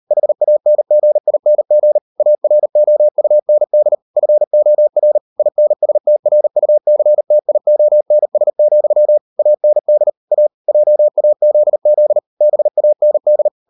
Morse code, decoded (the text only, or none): HANGING AROUND FOR INSTRUCTIONS, AND A JAZZ BAND